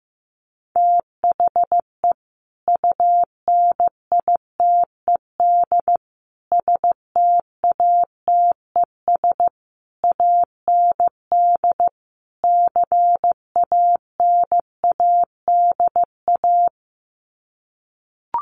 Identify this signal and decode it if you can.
Morse code, heard as a keyed tone, about 15 wpm, 700 Hz. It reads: THE UNITED STATES AND CANADA
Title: Sets of 5 Words 15wpm